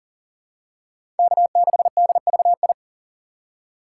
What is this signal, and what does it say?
Morse code, heard as a keyed tone, about 38 wpm, 700 Hz. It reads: K6DVI
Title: USA Call Signs - 5 Characters - 4x Character Spacing - 40wpm